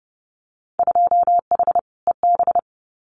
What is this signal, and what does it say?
Morse code, heard as a keyed tone, about 30 wpm, 700 Hz. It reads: 2H EB
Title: Winter Field Day Exchanges Encore 30wpm